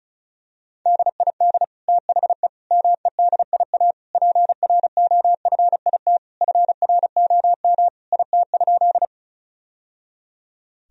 Morse code, heard as a keyed tone, about 36 wpm, 700 Hz. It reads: DID THE MEDIA PROFIT FROM IT?
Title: Sentences from Top 1000 Words - Review Part 4 - 35wpm